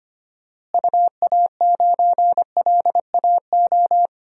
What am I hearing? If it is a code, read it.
Morse code, heard as a keyed tone, about 25 wpm, 700 Hz. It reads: UA9LAO
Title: DXCC-100 Call Signs - 6 Characters - Repeated 2x - 25wpm